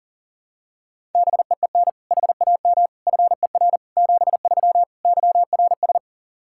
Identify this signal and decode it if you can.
Morse code, heard as a keyed tone, about 40 wpm, 700 Hz. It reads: BEEN HAM FER 73 YRS
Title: Practice QSO - Exchange 3 - More Info - 40wpm